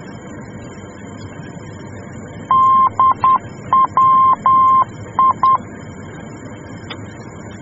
Morse code, heard as a keyed tone, about 10 wpm, 1050 Hz. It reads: DWI
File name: germany / mx / DWI-D